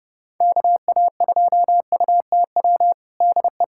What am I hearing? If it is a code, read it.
Morse code, heard as a keyed tone, about 30 wpm, 700 Hz. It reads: KA2UTW DE
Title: USA Callsign DE Callsign 30wpm